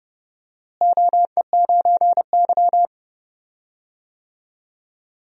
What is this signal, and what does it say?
Morse code, heard as a keyed tone, about 30 wpm, 700 Hz. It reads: OE9Y
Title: DXCC-100 Call Signs - 4 Characters - Repeated 2x - 30wpm